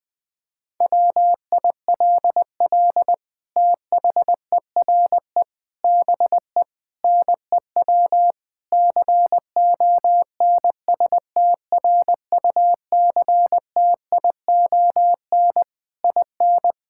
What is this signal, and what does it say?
Morse code, heard as a keyed tone, about 20 wpm, 700 Hz. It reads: WILL THERE BE NEW CONSTRUCTION IN
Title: Sentences from Top 1600 Words 20wpm